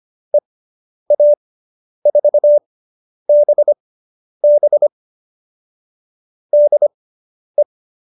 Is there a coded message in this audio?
Yes: Morse code from 0.3 to 7.6 s, heard as a keyed tone, about 25 wpm, spaced as for 10 wpm, 590 Hz. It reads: EA4BB DE